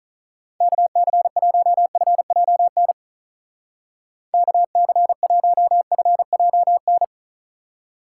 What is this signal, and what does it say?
Morse code, heard as a keyed tone, about 45 wpm, 700 Hz. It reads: KC1FJN KC1FJN